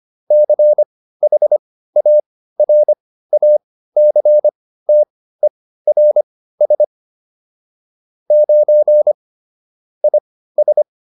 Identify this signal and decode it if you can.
Morse code, heard as a keyed tone, about 25 wpm, spaced as for 15 wpm, 600 Hz. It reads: CHARACTERS 9 IS